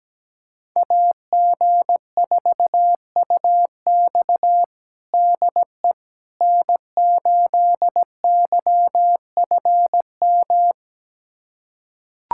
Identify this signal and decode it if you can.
Morse code, heard as a keyed tone, about 17 wpm, 700 Hz. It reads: AG4UX DE N8YFM
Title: USA Callsign DE Callsign - 5 Characters - 17wpm